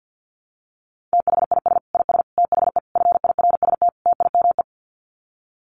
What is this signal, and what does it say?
Morse code, heard as a keyed tone, about 50 wpm, 700 Hz. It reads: THIS IS THE FIRST TIME
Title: Sets of 5 Words 50wpm